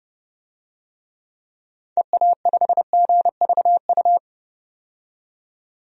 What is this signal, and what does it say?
Morse code, heard as a keyed tone, about 31 wpm, 700 Hz. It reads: EA5GVU
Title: DXCC-100 Call Signs - 6 Characters - 30wpm